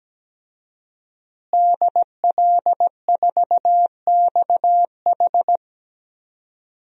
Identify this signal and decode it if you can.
Morse code, heard as a keyed tone, about 17 wpm, 700 Hz. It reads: DL4XH
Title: DXCC-100 Call Signs - 5 Characters - Repeated 3x - 17wpm